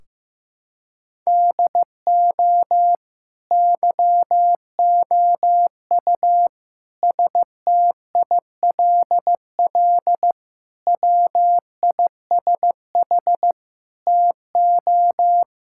Morse code, heard as a keyed tone, about 15 wpm, 700 Hz. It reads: DO YOU STILL WISH TO
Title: Sentences from Top 1600 Words 15wpm